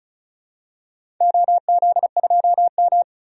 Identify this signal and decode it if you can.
Morse code, heard as a keyed tone, about 34 wpm, 700 Hz. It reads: OZ2M
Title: DXCC-100 Call Signs - 4 Characters - 35wpm